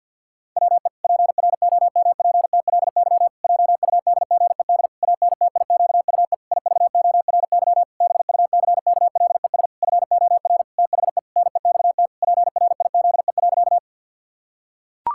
Morse code, heard as a keyed tone, about 45 wpm, 700 Hz. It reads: WE PROMPTLY JUDGED ANTIQUE IVORY BUCKLES FOR THE NEXT PRIZE.
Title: Pangram Challenge v2 - 50wpm